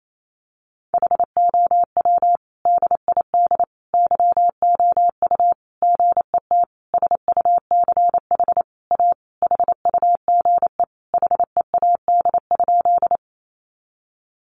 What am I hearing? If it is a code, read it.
Morse code, heard as a keyed tone, about 28 wpm, 700 Hz. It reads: HOW DID YOU GET SUCH A HUGE HEAD?
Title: Sentences from Top 1100 Words 28wpm